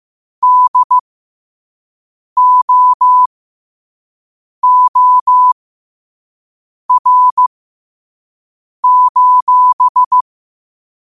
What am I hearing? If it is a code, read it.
Morse code, heard as a keyed tone, about 15 wpm, spaced as for 5 wpm, 1000 Hz. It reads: DOOR:
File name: qotdfiles / 5wpm